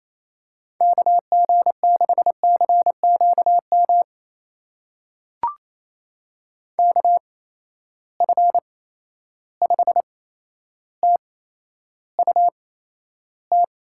Morse code, heard as a keyed tone, about 28 wpm, spaced as for 12 wpm, 700 Hz. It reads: KG6CQM KF5TUT
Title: USA Call Signs - 8x Character Spacing - 28wpm